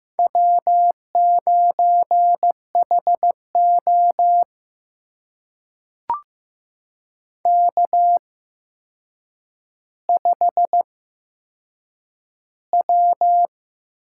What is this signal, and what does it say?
Morse code, heard as a keyed tone, about 15 wpm, spaced as for 7 wpm, 700 Hz. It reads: W9HO K5W
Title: USA Call Signs - 4 Characters - 8x Character Spacing - 15wpm